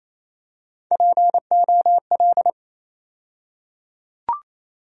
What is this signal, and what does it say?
Morse code, heard as a keyed tone, about 28 wpm, 700 Hz. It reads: POL
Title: USA Call Sign Postfixes - 3 Letters 28wpm